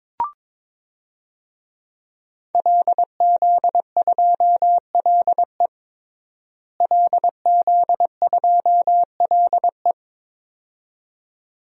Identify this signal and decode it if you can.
Morse code, heard as a keyed tone, about 22 wpm, 700 Hz. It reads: LZ2LE LZ2LE
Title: DXCC-100 Call Signs - 5 Characters - Repeated 2x - 22wpm